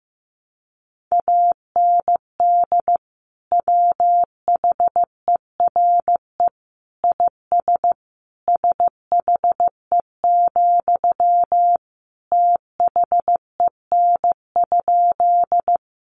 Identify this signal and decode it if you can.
Morse code, heard as a keyed tone, about 15 wpm, 700 Hz. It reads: AND WHERE IS SHE, THEN?
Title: Sentences from Top 200 Words 15wpm